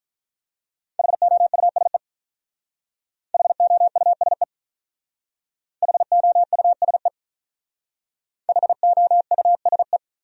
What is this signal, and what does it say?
Morse code, heard as a keyed tone, about 45 wpm, 700 Hz. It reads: HOUSE HOUSE HOUSE HOUSE